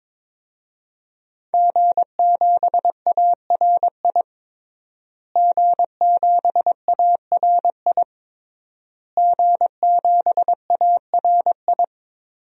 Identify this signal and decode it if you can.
Morse code, heard as a keyed tone, about 22 wpm, 700 Hz. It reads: G7ARI G7ARI G7ARI
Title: DXCC-100 Call Signs - 5 Characters - Repeated 3x - 22wpm